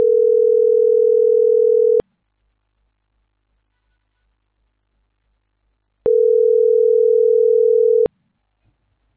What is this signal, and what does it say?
Ringback tone (North America), 440 + 480 Hz.